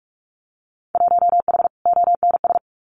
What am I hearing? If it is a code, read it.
Morse code, heard as a keyed tone, about 46 wpm, 690 Hz. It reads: 1H ONS
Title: Winter Field Day Exchanges Encore 45wpm